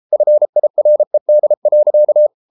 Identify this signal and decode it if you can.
Morse code, heard as a keyed tone, about 33 wpm, 600 Hz. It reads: FIRED.